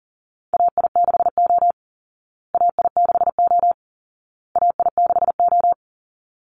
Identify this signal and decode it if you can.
Morse code, heard as a keyed tone, about 40 wpm, 700 Hz. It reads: AI6O AI6O AI6O
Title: Call Signs Repeated 2x - 40wpm